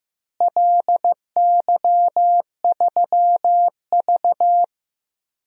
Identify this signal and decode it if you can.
Morse code, heard as a keyed tone, about 15 wpm, 700 Hz. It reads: LY3V
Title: DXCC-100 Call Signs - 4 Characters - Repeated 2x - 15wpm